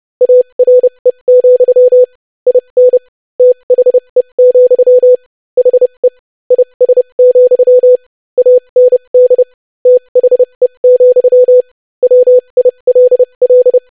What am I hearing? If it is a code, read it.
Morse code, heard as a keyed tone, about 30 wpm, spaced as for 25 wpm, 500 Hz. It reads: ARE, IN THE, HE IS, AND THE, WILL
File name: lesson / basicphrase-25wpm